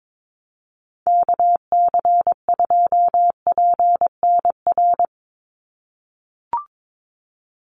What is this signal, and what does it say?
Morse code, heard as a keyed tone, about 22 wpm, 700 Hz. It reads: KC2PNR
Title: USA Call Signs 22wpm